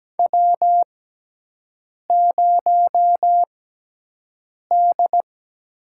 Morse code, heard as a keyed tone, about 17 wpm, spaced as for 6 wpm, 700 Hz. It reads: W0D
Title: USA Call Signs - 5 Characters - 6x Character Spacing - 17wpm